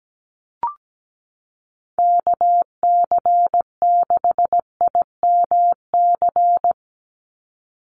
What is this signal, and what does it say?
Morse code, heard as a keyed tone, about 17 wpm, 700 Hz. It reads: KC6IMC
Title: USA Call Signs 17wpm